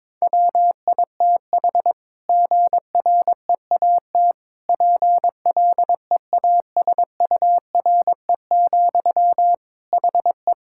Morse code, heard as a keyed tone, about 22 wpm, 700 Hz. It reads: WITH GREAT PLEASURE, HE
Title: Sentences from Top 1600 Words 22wpm